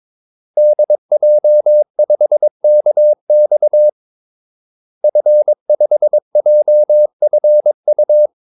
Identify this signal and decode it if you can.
Morse code, heard as a keyed tone, about 22 wpm, 600 Hz. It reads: DJ5KX F5JFU
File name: Callsigns-22wpm